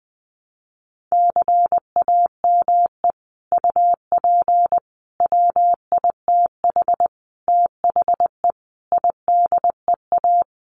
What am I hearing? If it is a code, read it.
Morse code, heard as a keyed tone, about 20 wpm, 700 Hz. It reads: CAME UP WITH THE IDEA
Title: Sets of 5 Words Encore2 20wpm